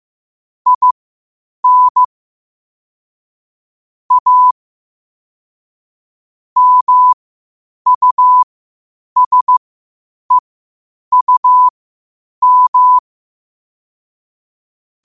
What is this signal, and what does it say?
Morse code, heard as a keyed tone, about 15 wpm, spaced as for 8 wpm, 1000 Hz. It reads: IN A MUSEUM
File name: qotdfiles / 8wpm